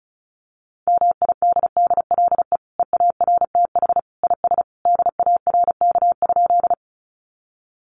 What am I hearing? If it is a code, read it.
Morse code, heard as a keyed tone, about 34 wpm, 700 Hz. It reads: MIDDLE EARTH IS DARK?
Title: Sentences from Top 900 Words 35wpm